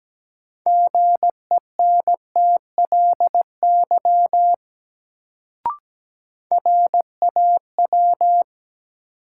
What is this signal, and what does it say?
Morse code, heard as a keyed tone, about 17 wpm, 700 Hz. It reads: GENTLY RAW
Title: Sentences from Top 1500 Words - New Word Drill - 17wpm